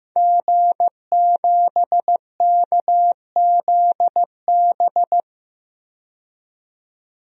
Morse code, heard as a keyed tone, about 15 wpm, 700 Hz. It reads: G7KZB
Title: DXCC-100 Call Signs - 5 Characters - 15wpm